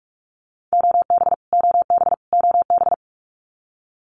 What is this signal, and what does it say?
Morse code, heard as a keyed tone, about 44 wpm, 700 Hz. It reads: OB OB OB
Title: Abbreviations Repeated 3x - 45wpm